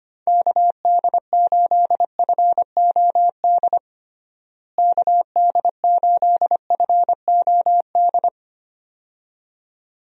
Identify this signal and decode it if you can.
Morse code, heard as a keyed tone, about 25 wpm, 700 Hz. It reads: KD8FOD KD8FOD
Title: USA Call Signs - 6 Characters - Repeated 2x - 25wpm